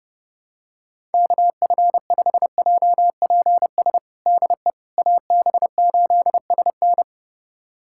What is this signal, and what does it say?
Morse code, heard as a keyed tone, about 30 wpm, 700 Hz. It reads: KF5JPS DE AB8SN
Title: USA Callsign DE Callsign 30wpm